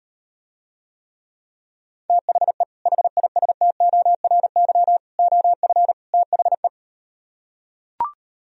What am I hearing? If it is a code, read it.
Morse code, heard as a keyed tone, about 38 wpm, 700 Hz. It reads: THE HISTORY OF THE